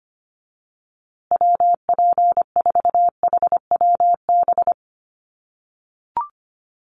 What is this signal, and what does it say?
Morse code, heard as a keyed tone, about 25 wpm, 700 Hz. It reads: WP4HWB
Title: USA Call Signs Repeated 2x 25wpm